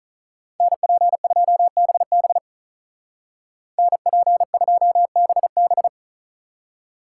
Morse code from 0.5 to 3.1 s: NP2BB

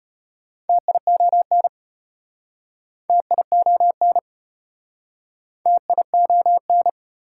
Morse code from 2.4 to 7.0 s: TION TION

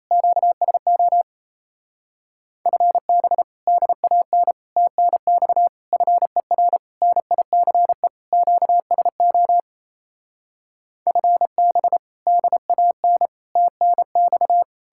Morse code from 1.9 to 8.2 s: FB DAN TNX FER NICE

